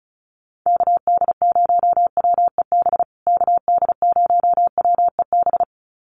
Morse code, heard as a keyed tone, about 34 wpm, 700 Hz. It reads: KD0WEB KD0WEB
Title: USA Call Signs Repeated 2x 35wpm